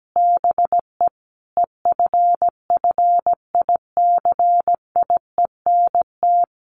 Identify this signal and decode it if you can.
Morse code, heard as a keyed tone, about 17 wpm, 700 Hz. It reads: BE EFFICIENT
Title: Sentences from Top 1300 Words 17wpm